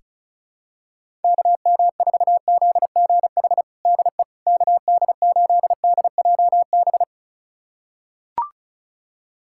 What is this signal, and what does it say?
Morse code, heard as a keyed tone, about 34 wpm, 700 Hz. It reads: KM4ZGH DE KD8DJB
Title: USA Callsign DE Callsign 35wpm